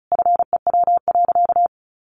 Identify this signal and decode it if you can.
Morse code, heard as a keyed tone, about 34 wpm, 700 Hz. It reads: FEW.
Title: Sentences from Top 1500 Words 35wpm